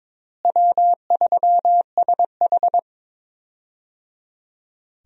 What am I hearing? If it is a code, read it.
Morse code, heard as a keyed tone, about 22 wpm, 700 Hz. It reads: W3SH
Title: USA Call Signs - 4 Characters - Repeated 3x - 22wpm